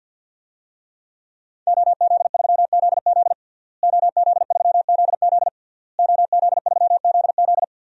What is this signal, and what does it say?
Morse code, heard as a keyed tone, about 51 wpm, 700 Hz. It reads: OZ3ZZ OZ3ZZ OZ3ZZ
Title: DXCC-100 Call Signs - 5 Characters - Repeated 3x - 50wpm